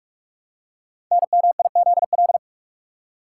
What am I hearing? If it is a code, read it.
Morse code, heard as a keyed tone, about 44 wpm, 700 Hz. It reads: NMIZL
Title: Code Groups - 5 Letters - 45wpm